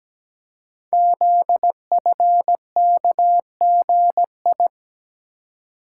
Morse code, heard as a keyed tone, about 17 wpm, 700 Hz. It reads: ZFKGI
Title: Code Groups - 5 Letters - 17wpm